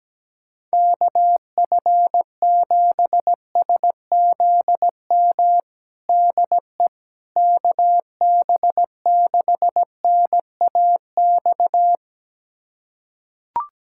Morse code, heard as a keyed tone, about 17 wpm, 700 Hz. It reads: KF7SZM DE KB6NAX
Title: USA Callsign DE Callsign - 6 Characters - 17wpm